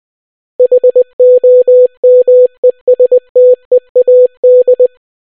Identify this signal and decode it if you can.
Morse code, heard as a keyed tone, about 20 wpm, 500 Hz. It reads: HOMESTEAD